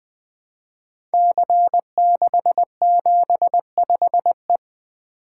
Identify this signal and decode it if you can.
Morse code, heard as a keyed tone, about 20 wpm, 700 Hz. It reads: C675E
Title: Code Groups - 5 Letter:Numbers - 20wpm